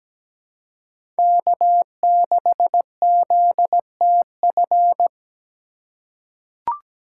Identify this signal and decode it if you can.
Morse code, heard as a keyed tone, about 17 wpm, 700 Hz. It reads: K6ZTF
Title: USA Call Signs - 5 Characters - 4x Character Spacing - 17wpm